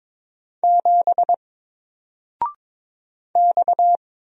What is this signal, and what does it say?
Morse code, heard as a keyed tone, about 22 wpm, 700 Hz. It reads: 7 X
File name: Single-Letter-Number-22wpm